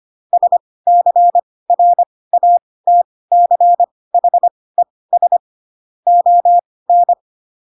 Morse code, heard as a keyed tone, about 25 wpm, spaced as for 18 wpm, 700 Hz. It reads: SCRATCHES ON
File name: Pinocchio / Pinocchio ch2 12 25 ik1hge